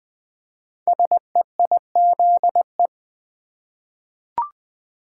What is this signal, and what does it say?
Morse code, heard as a keyed tone, about 20 wpm, 700 Hz. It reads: SEIZE
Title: 5 Letter Words - First 3 Letters Given - 20wpm